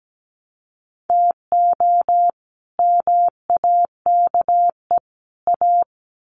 Morse code, heard as a keyed tone, about 17 wpm, 700 Hz. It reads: TO MAKE A
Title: Sets of 3 Words 17wpm